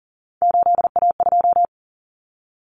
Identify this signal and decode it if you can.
Morse code, heard as a keyed tone, about 40 wpm, 700 Hz. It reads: 8A2
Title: Common Cut Numbers - 3-Digit 40wpm